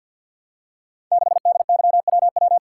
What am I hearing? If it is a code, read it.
Morse code, heard as a keyed tone, about 52 wpm, 700 Hz. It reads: 6DYWW
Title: Code Groups - 5 Letter:Numbers - 50wpm